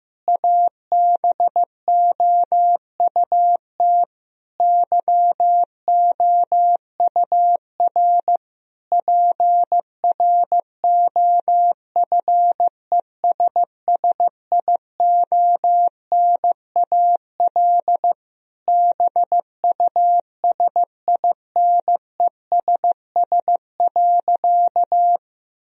Morse code, heard as a keyed tone, about 16 wpm, 700 Hz. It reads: ABOUT YOUR PROFESSIONAL BUSINESS.
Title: Sentences from Top 1000 Words - Review Part 4 - 15wpm